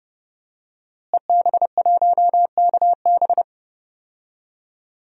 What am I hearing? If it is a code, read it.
Morse code, heard as a keyed tone, about 30 wpm, 700 Hz. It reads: EB1KB
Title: DXCC-100 Call Signs - 5 Characters - Repeated 3x - 30wpm